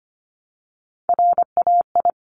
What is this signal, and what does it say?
Morse code, heard as a keyed tone, about 25 wpm, 700 Hz. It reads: RAI
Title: Multi-letter Phonemes Repeated 3x 25wpm